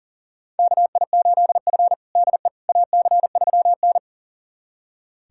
Morse code, heard as a keyed tone, about 40 wpm, 700 Hz. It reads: KI8F DE AC3N
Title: USA Callsign DE Callsign - 4 Characters - 40wpm